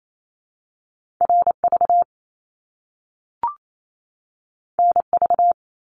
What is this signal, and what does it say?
Morse code, heard as a keyed tone, about 28 wpm, 700 Hz. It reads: RV NV
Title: Top Two Letter Combinations in Words 28wpm